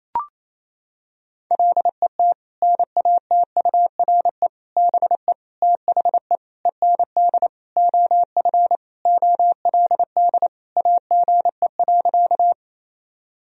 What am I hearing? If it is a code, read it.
Morse code, heard as a keyed tone, about 28 wpm, 700 Hz. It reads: LET NATURE BE THE END OF OLD AGE.
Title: Sentences from Top 1000 Words - Review Part 3 - 28wpm